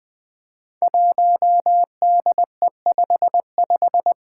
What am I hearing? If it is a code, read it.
Morse code, heard as a keyed tone, about 20 wpm, 700 Hz. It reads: 1DE55
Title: Five Challenging Letter-Number Combinations 20wpm